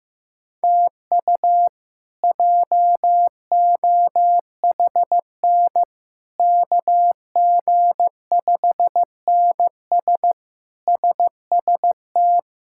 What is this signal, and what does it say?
Morse code, heard as a keyed tone, about 15 wpm, 700 Hz. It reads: TU JOHN KG5NS SST
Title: K1USN SST QSO Practice - 15wpm